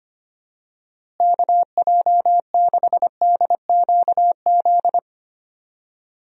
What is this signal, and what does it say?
Morse code, heard as a keyed tone, about 25 wpm, 700 Hz. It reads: KJ6DQZ
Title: USA Call Signs - 6 Characters - Repeated 2x - 25wpm